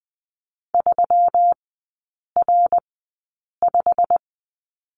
Morse code, heard as a keyed tone, about 20 wpm, spaced as for 8 wpm, 700 Hz. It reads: 3R5
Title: 121 Lesson 20e - Number 3 - Three Characters 20wpm